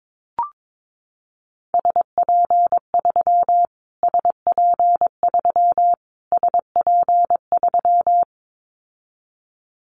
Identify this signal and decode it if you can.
Morse code, heard as a keyed tone, about 22 wpm, 700 Hz. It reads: SP3 SP3 SP3
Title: Callsign Prefixes for Top 25 DXCC - Repeated 3x - 22wpm